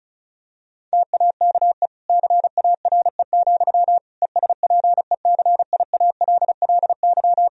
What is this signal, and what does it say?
Morse code, heard as a keyed tone, about 36 wpm, 700 Hz. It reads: TAKE CARE, ESPECIALLY
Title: Sentences from Top 1000 Words - Review Part 4 - 35wpm